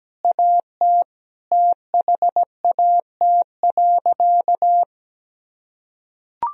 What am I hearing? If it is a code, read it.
Morse code, heard as a keyed tone, about 17 wpm, 700 Hz. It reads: AT THAT.
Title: Sentences from Top 1000 Words - Review Part 1 - 17wpm